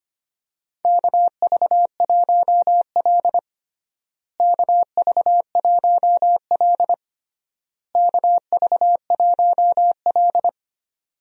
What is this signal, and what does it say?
Morse code, heard as a keyed tone, about 25 wpm, 700 Hz. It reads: KV1L KV1L KV1L
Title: USA Call Signs - 4 Characters - Repeated 3x - 25wpm